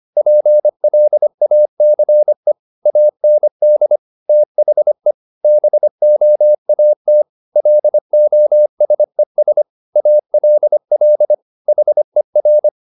Morse code, heard as a keyed tone, about 25 wpm, 600 Hz. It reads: PLACE AND THE BOAT LOSES ALL HER